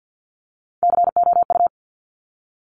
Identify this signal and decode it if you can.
Morse code, heard as a keyed tone, about 50 wpm, 700 Hz. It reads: COU
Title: Top Three Letter Combinations in Words 50wpm